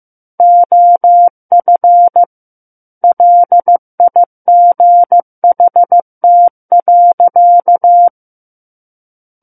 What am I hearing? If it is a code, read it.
Morse code, heard as a keyed tone, about 15 wpm, 700 Hz. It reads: OF LIGHT.
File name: Chapter 6 - 15 WPM